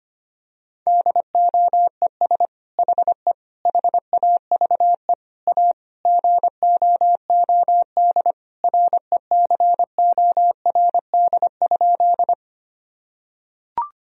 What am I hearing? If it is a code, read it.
Morse code, heard as a keyed tone, about 25 wpm, 700 Hz. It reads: DOES HE HAVE A GOOD RECORD?
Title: Sentences from Top 1000 Words - Review Part 3 - 25wpm